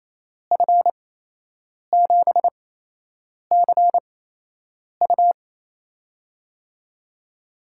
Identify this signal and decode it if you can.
Morse code, heard as a keyed tone, about 28 wpm, spaced as for 8 wpm, 700 Hz. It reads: F7CU
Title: USA Call Signs - 5 Characters - 8x Character Spacing - 28wpm